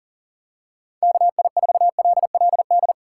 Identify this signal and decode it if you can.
Morse code, heard as a keyed tone, about 40 wpm, 700 Hz. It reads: KI4LLD
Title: USA Call Signs - 6 Characters - 6x Character Spacing - 40wpm